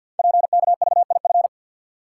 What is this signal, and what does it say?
Morse code, heard as a keyed tone, about 48 wpm, 690 Hz. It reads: PKVIF
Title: Code Groups - 5 Letters - 50wpm